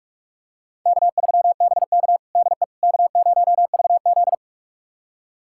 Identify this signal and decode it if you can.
Morse code, heard as a keyed tone, about 46 wpm, 700 Hz. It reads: K3BK DE K0VZ
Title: USA Callsign DE Callsign - 4 Characters - 45wpm